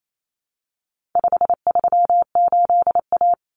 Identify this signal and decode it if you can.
Morse code, heard as a keyed tone, about 28 wpm, 700 Hz. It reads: 538A
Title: Common Cut Numbers - 4-Digit 28wpm